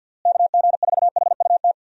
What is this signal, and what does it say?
Morse code, heard as a keyed tone, about 52 wpm, 700 Hz. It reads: KG4HUT
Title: USA Call Signs - 6 Characters - Repeated 3x - 50wpm